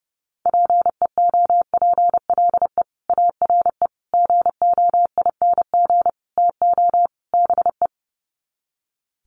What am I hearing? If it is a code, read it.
Morse code, heard as a keyed tone, about 28 wpm, 700 Hz. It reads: PEOPLE ARE GOING TO BE
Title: Sets of 5 Words Encore 30wpm